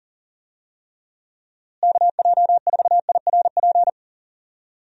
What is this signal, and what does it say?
Morse code, heard as a keyed tone, about 37 wpm, 700 Hz. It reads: KJ4IRP
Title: USA Call Signs - 6 Characters - Repeated 3x - 40wpm